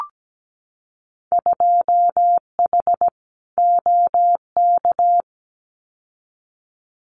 Morse code, heard as a keyed tone, about 17 wpm, 700 Hz. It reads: 2H OK
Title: Winter Field Day Exchanges Encore 17wpm